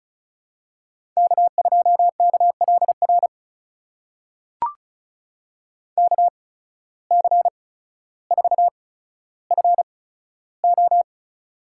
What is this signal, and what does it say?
Morse code, heard as a keyed tone, about 34 wpm, spaced as for 15 wpm, 700 Hz. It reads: K2KLR KC4FO
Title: USA Call Signs - 5 Characters - 8x Character Spacing - 35wpm